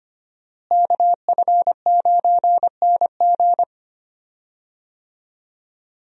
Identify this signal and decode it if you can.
Morse code, heard as a keyed tone, about 25 wpm, 700 Hz. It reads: KF9NG
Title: USA Call Signs - 5 Characters - Repeated 3x - 25wpm